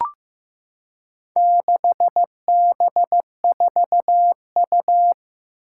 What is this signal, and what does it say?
Morse code, heard as a keyed tone, about 15 wpm, 700 Hz. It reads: 6B4U
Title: Four Challenging Letter-Number Combinations 15wpm